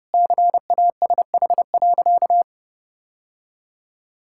Morse code, heard as a keyed tone, about 30 wpm, 700 Hz. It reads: CASH.
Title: Sentences from Top 1000 Words - Review Part 5 - 30wpm